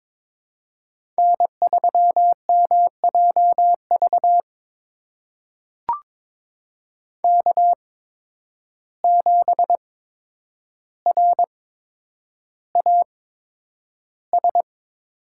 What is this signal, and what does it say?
Morse code, heard as a keyed tone, about 22 wpm, spaced as for 9 wpm, 700 Hz. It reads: N3MJV K7RAS
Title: USA Call Signs - 5 Characters - 8x Character Spacing - 22wpm